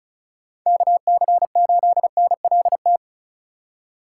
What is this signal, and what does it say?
Morse code, heard as a keyed tone, about 34 wpm, 700 Hz. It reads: KC8NLT
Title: USA Call Signs - 6 Characters - 3x Character Spacing - 35wpm